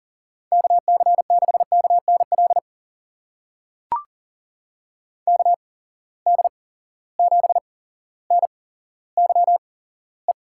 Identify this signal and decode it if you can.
Morse code, heard as a keyed tone, about 40 wpm, spaced as for 17 wpm, 700 Hz. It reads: KC6KNL KD7NYE
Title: USA Call Signs - 6 Characters - 8x Character Spacing - 40wpm